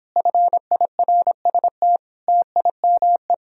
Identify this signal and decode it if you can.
Morse code, heard as a keyed tone, about 31 wpm, 700 Hz. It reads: FIRST TIME